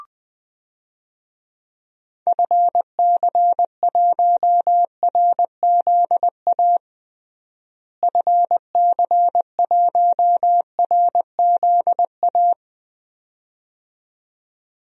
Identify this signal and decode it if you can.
Morse code, heard as a keyed tone, about 20 wpm, 700 Hz. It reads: FC1RZA FC1RZA
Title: DXCC-100 Call Signs - 6 Characters - Repeated 2x - 20wpm